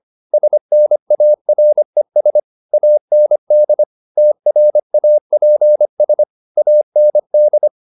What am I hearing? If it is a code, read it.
Morse code, heard as a keyed tone, about 25 wpm, 600 Hz. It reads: SNARES AND TRAPS AND
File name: the old man and the sea part 1 25wpm